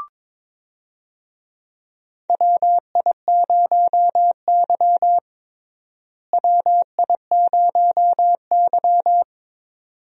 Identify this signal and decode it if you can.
Morse code, heard as a keyed tone, about 21 wpm, 700 Hz. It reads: WI0Y WI0Y
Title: USA Call Signs - 4 Characters - Repeated 2x - 22wpm